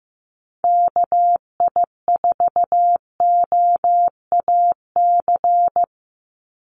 Morse code, heard as a keyed tone, about 15 wpm, 700 Hz. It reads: KI4OAC
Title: USA Call Signs 15wpm